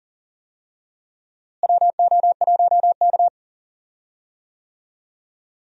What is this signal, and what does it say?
Morse code, heard as a keyed tone, about 40 wpm, 700 Hz. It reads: WO1K
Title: USA Call Signs - 4 Characters - Repeated 2x - 40wpm